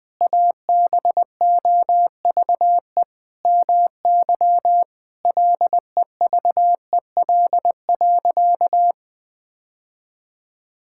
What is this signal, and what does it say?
Morse code, heard as a keyed tone, about 21 wpm, 700 Hz. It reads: ABOVE MY LEVEL.
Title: Sentences from Top 1000 Words - Review Part 6 - 20wpm